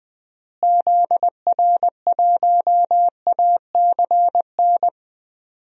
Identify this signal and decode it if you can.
Morse code, heard as a keyed tone, about 20 wpm, 700 Hz. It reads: ZR1ACN
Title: DXCC-100 Call Signs - 6 Characters - Repeated 3x - 20wpm